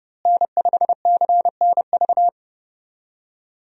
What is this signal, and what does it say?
Morse code, heard as a keyed tone, about 30 wpm, 700 Hz. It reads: N5CNV
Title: USA Call Signs - 5 Characters - Repeated 3x - 30wpm